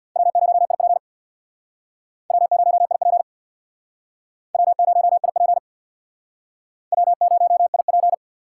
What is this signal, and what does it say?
Morse code, heard as a keyed tone, about 65 wpm, 700 Hz. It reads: W0IP W0IP W0IP W0IP